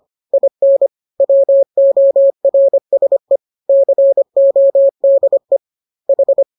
Morse code, heard as a keyed tone, about 25 wpm, 550 Hz. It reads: IN WORSE CODE H